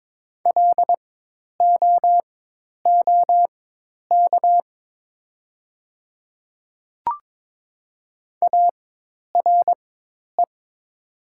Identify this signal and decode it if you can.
Morse code, heard as a keyed tone, about 23 wpm, spaced as for 10 wpm, 700 Hz. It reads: LOOK ARE
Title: 4x Character Spacing - Top 100 Words - 22wpm